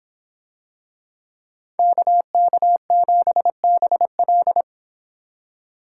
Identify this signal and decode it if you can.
Morse code, heard as a keyed tone, about 26 wpm, 700 Hz. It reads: KK7BL